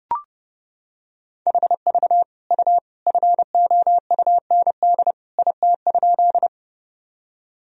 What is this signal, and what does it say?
Morse code, heard as a keyed tone, about 30 wpm, 700 Hz. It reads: HV U FOUND IT?
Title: Sentences from Top 300 Words and Abbreviations 30wpm